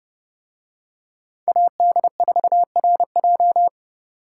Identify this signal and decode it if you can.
Morse code, heard as a keyed tone, about 28 wpm, 700 Hz. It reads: AD4RJ